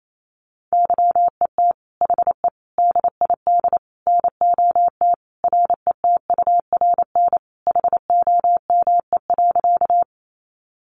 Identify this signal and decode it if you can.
Morse code, heard as a keyed tone, about 28 wpm, 700 Hz. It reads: YET HE DID NOT RETURN HOME.
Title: Sentences from Top 800 Words 28wpm